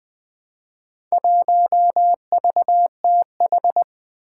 Morse code, heard as a keyed tone, about 20 wpm, 700 Hz. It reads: 1VTH
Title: Four Challenging Letter-Number Combinations 20wpm